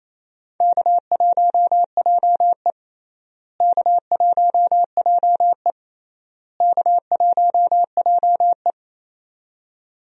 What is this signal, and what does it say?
Morse code, heard as a keyed tone, about 28 wpm, 700 Hz. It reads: K1JE K1JE K1JE
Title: USA Call Signs - 4 Characters - Repeated 3x - 28wpm